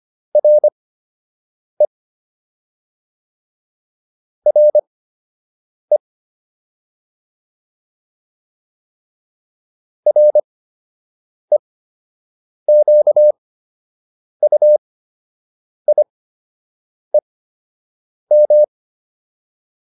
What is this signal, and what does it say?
Morse code, heard as a keyed tone, about 25 wpm, spaced as for 7 wpm, 600 Hz. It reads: RE RE REQUIEM